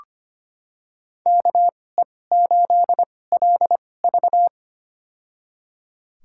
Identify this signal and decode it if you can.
Morse code, heard as a keyed tone, about 25 wpm, spaced as for 18 wpm, 700 Hz. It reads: KE8LV
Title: USA Call Signs - 5 Characters - 2x Character Spacing - 25wpm